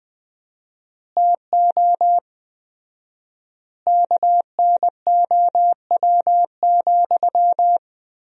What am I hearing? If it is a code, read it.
Morse code, heard as a keyed tone, about 21 wpm, 700 Hz. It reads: TO KNOW,
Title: 4x Word-Spacing - Sentences from Top 200 Words - 20wpm